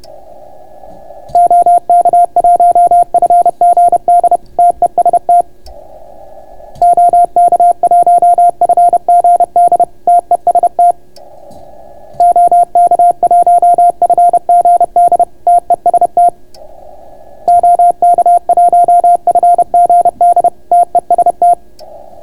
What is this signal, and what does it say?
Morse code, heard as a keyed tone, about 31 wpm, 700 Hz. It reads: OK1FGD TEST OK1FGD TEST OK1FGD TEST OK1FGD TEST